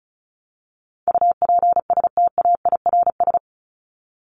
Morse code, heard as a keyed tone, about 34 wpm, 700 Hz. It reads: UPSTAIRS
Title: Top 1751-2000 Words 35wpm